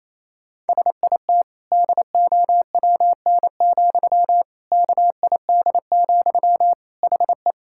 Morse code, heard as a keyed tone, about 28 wpm, 700 Hz. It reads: SIT DOWN, KID, HE
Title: Sentences from Top 1600 Words 28wpm